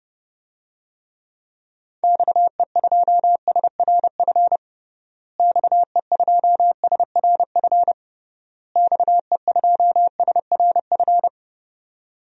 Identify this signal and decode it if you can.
Morse code, heard as a keyed tone, about 30 wpm, 700 Hz. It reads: XE2SRF XE2SRF XE2SRF
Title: DXCC-100 Call Signs - 6 Characters - Repeated 3x - 30wpm